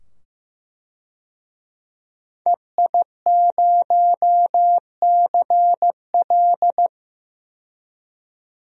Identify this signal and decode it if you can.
Morse code, heard as a keyed tone, about 15 wpm, 700 Hz. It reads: EI0CL
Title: DXCC-100 Call Signs - 5 Characters - 15wpm